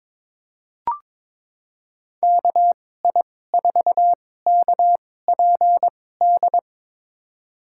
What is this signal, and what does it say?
Morse code, heard as a keyed tone, about 21 wpm, spaced as for 16 wpm, 700 Hz. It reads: KI4KPD
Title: USA Call Signs - 6 Characters - 2x Character Spacing - 22wpm